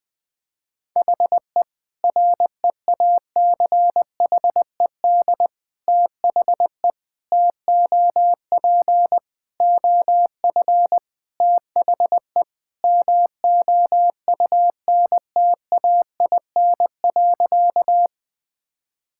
Morse code, heard as a keyed tone, about 21 wpm, 700 Hz. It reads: HE REACHED THE TOP OF THE MOUNTAIN.
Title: Sentences with 3-Word Phrases - Spoken Beginning - 20wpm